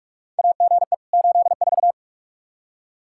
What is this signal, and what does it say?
Morse code, heard as a keyed tone, about 46 wpm, 700 Hz. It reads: AGE 84
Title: Practice QSO - Exchange 3 - More Info - 45wpm